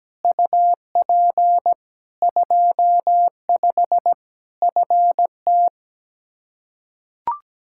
Morse code, heard as a keyed tone, about 17 wpm, 700 Hz. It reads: UP 25 FT
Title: K1USN SST QSO Practice 17wpm